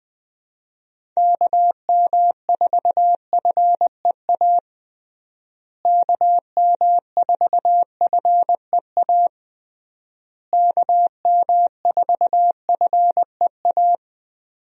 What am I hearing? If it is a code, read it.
Morse code, heard as a keyed tone, about 21 wpm, 700 Hz. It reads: KM4FEA KM4FEA KM4FEA
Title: USA Call Signs - 6 Characters - Repeated 3x - 20wpm